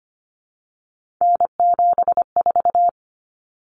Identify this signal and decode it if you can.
Morse code, heard as a keyed tone, about 25 wpm, 700 Hz. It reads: N74
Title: Common Cut Numbers - 3-Digit 25wpm